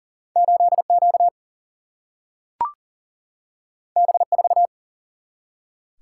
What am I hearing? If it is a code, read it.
Morse code, heard as a keyed tone, about 41 wpm, 700 Hz. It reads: 8Q B4
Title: Single-Letter Single-Number 40wpm